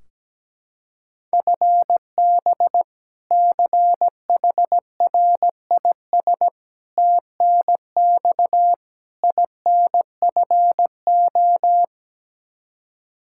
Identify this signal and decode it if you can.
Morse code, heard as a keyed tone, about 17 wpm, 700 Hz. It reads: FB CHRIS TNX INFO
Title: Practice QSO - Exchange 3 - More Info - 17wpm